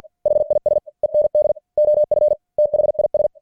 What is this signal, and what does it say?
Morse code, heard as a keyed tone, about 50 wpm, 600 Hz. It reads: HIS END OF THIS
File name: archived / CWBros